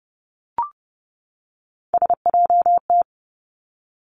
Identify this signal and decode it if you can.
Morse code, heard as a keyed tone, about 30 wpm, 700 Hz. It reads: SJT
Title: Three Challenging Letter-Number Combinations 30wpm